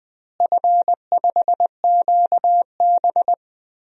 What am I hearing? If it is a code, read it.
Morse code, heard as a keyed tone, about 20 wpm, 700 Hz. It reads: F5QB
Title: DXCC-100 Call Signs - 4 Characters - Repeated 3x - 20wpm